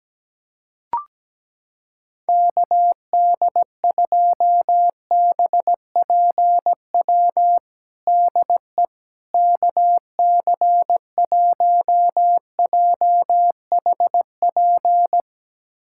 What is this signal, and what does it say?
Morse code, heard as a keyed tone, about 17 wpm, 700 Hz. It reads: KD2BPW DE KC1JHP
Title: USA Callsign DE Callsign 17wpm